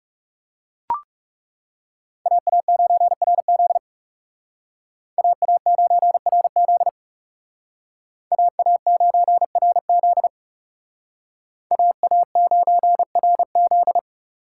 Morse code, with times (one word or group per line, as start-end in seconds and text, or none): 2.3-3.8 AA9RZ
5.2-6.9 AA9RZ
8.3-10.3 AA9RZ
11.7-14.0 AA9RZ